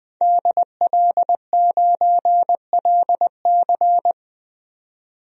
Morse code, heard as a keyed tone, about 20 wpm, 700 Hz. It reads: DL9LC
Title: DXCC-100 Call Signs - 5 Characters - 20wpm